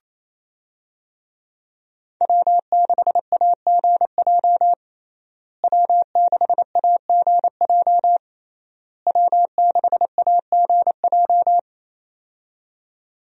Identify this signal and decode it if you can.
Morse code, heard as a keyed tone, about 28 wpm, 700 Hz. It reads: W6AGJ W6AGJ W6AGJ
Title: USA Call Signs - 5 Characters - Repeated 3x - 28wpm